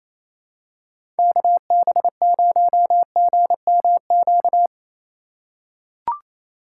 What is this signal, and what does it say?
Morse code, heard as a keyed tone, about 28 wpm, 700 Hz. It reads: KB0GMQ
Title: USA Call Signs - 6 Characters - 4x Character Spacing - 28wpm